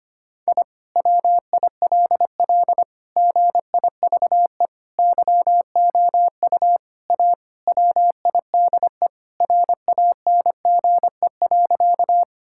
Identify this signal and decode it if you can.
Morse code, heard as a keyed tone, about 25 wpm, 700 Hz. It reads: I WILL GIVE YOU A WIDE RANGE.
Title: Sentences from Top 1000 Words - Review Part 6 - 25wpm